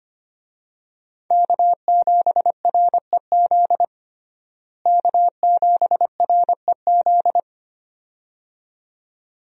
Morse code, heard as a keyed tone, about 25 wpm, 700 Hz. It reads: K7REZ K7REZ
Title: USA Call Signs - 5 Characters - Repeated 2x - 25wpm